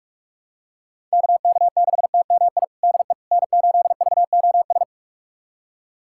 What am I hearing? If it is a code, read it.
Morse code, heard as a keyed tone, about 46 wpm, 700 Hz. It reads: KK6TMI DE N8VOS
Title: USA Callsign DE Callsign 45wpm